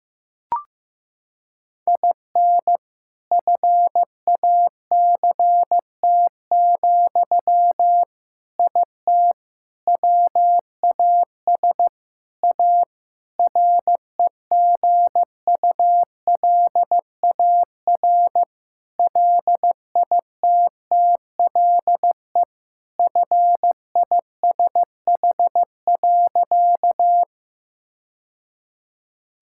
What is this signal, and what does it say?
Morse code, heard as a keyed tone, about 15 wpm, 700 Hz. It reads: IN FACT, IT WAS A REGULAR LITTLE FISH.
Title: Sentences from Top 1000 Words - Review Part 6 - 15wpm